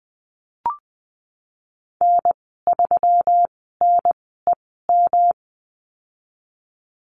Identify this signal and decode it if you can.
Morse code, heard as a keyed tone, about 19 wpm, spaced as for 14 wpm, 700 Hz. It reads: N3NEM
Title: 128 Lesson 21g - Number 6 - Call Signs 20wpm